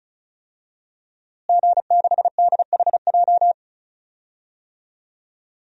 Morse code, heard as a keyed tone, about 34 wpm, 700 Hz. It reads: G6DHJ
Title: DXCC-100 Call Signs - 5 Characters - Repeated 3x - 35wpm